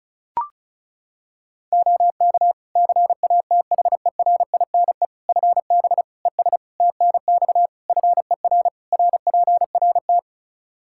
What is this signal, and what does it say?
Morse code, heard as a keyed tone, about 36 wpm, 700 Hz. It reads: OK CATHERINE FB ES TNX FER RPRT
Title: Practice QSO - Exchange 2 - Info - 35wpm